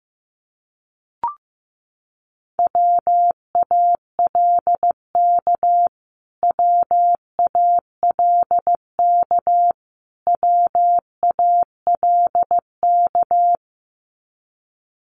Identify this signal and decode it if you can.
Morse code, heard as a keyed tone, about 15 wpm, 700 Hz. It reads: WALK WALK WALK
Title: Top 500 Words Repeated 15wpm